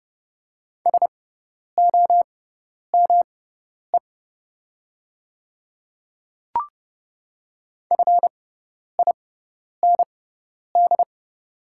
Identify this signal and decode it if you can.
Morse code, heard as a keyed tone, about 31 wpm, spaced as for 10 wpm, 690 Hz. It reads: SOME FIND
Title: 6x Character Spacing - Top 100 Words - 30wpm